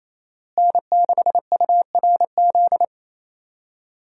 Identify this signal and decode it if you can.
Morse code, heard as a keyed tone, about 28 wpm, 700 Hz. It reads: N6URZ
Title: USA Call Signs - 5 Characters - Repeated 2x - 28wpm